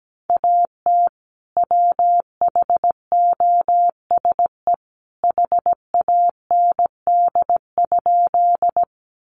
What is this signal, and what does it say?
Morse code, heard as a keyed tone, about 17 wpm, 700 Hz. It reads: AT WHOSE HAND?
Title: Sentences from Top 1200 Words 17wpm